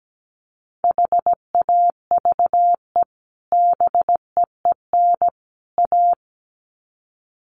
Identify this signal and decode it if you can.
Morse code, heard as a keyed tone, about 17 wpm, 700 Hz. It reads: HAVE BEEN A
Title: Sets of 3 Words 17wpm